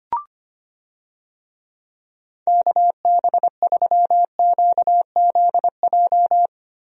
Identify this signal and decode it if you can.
Morse code, heard as a keyed tone, about 25 wpm, 700 Hz. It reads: KB3QZJ
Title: USA Call Signs - 6 Characters - Repeated 3x - 25wpm